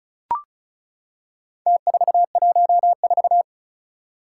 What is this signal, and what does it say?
Morse code, heard as a keyed tone, about 36 wpm, 700 Hz. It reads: T414
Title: Four Challenging Letter-Number Combinations 35wpm